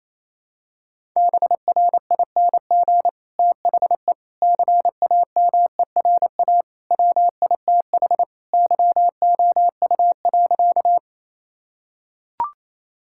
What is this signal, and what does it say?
Morse code, heard as a keyed tone, about 28 wpm, 700 Hz. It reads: BRING THE CAMERA WITH YOU.
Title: Sentences from Top 1000 Words - Review Part 5 - 28wpm